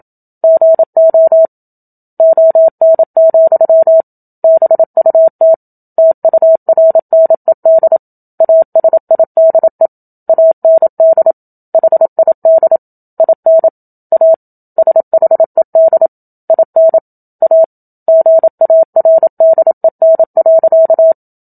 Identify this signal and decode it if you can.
Morse code, heard as a keyed tone, about 27 wpm, 650 Hz. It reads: GO ON, BUT TURNED ASIDE AND HID IN A SHED IN A GARDEN.
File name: Book Two / 01-Chapter 1 - 27 WPM